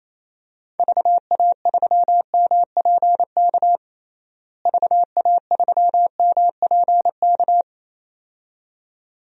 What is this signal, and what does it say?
Morse code, heard as a keyed tone, about 28 wpm, 700 Hz. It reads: VA3MPK VA3MPK
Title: DXCC-100 Call Signs - 6 Characters - Repeated 2x - 28wpm